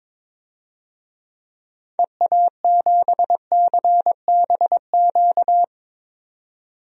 Morse code, heard as a keyed tone, about 22 wpm, 700 Hz. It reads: EA7CBQ
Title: DXCC-100 Call Signs - 6 Characters - 22wpm